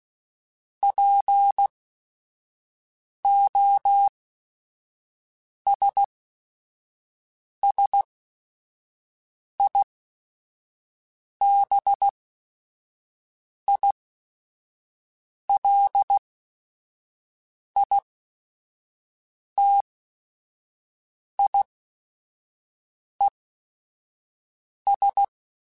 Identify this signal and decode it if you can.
Morse code, heard as a keyed tone, about 16 wpm, spaced as for 5 wpm, 800 Hz. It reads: POSSIBILITIES